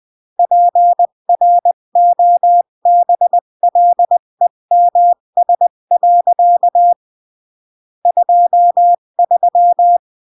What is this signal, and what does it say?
Morse code, heard as a keyed tone, about 20 wpm, 700 Hz. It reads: PROBLEMS. 23
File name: CW / jodorowsky / 82 commandments 20-15